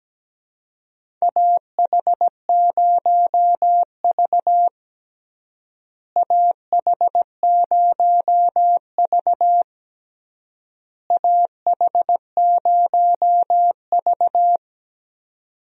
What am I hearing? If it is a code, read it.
Morse code, heard as a keyed tone, about 17 wpm, 700 Hz. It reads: AH0V AH0V AH0V
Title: USA Call Signs - 4 Characters - Repeated 3x - 17wpm